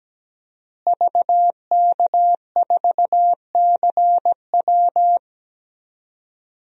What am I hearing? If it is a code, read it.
Morse code, heard as a keyed tone, about 17 wpm, 700 Hz. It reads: VK4CW
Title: DXCC-100 Call Signs - 5 Characters - Repeated 2x - 17wpm